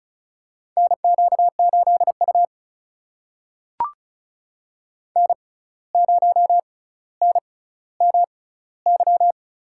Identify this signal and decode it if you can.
Morse code, heard as a keyed tone, about 36 wpm, spaced as for 17 wpm, 700 Hz. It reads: NQ8U N0NMY
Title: USA Call Signs - 6x Character Spacing - 35wpm